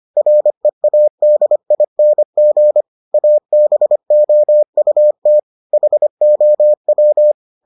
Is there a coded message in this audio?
Yes: Morse code from 0.2 to 7.3 s, heard as a keyed tone, about 25 wpm, 600 Hz. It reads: READING ABOUT HOW